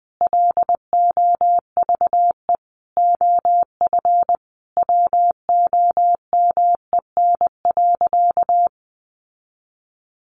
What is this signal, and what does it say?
Morse code, heard as a keyed tone, about 20 wpm, 700 Hz. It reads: LOVE OF WOMEN.
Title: Sentences from Top 1400 Words 20wpm